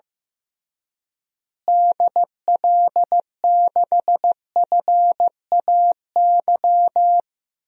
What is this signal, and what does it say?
Morse code, heard as a keyed tone, about 15 wpm, 700 Hz. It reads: DL6FAY
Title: DXCC-100 Call Signs - 6 Characters - Repeated 3x - 15wpm